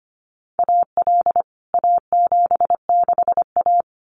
Morse code, heard as a keyed tone, about 25 wpm, 700 Hz. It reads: AL A76A
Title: CWT Exchange 25wpm